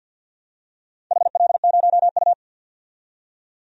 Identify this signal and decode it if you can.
Morse code, heard as a keyed tone, about 51 wpm, 700 Hz. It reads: HL0U
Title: DXCC-100 Call Signs - 4 Characters - 50wpm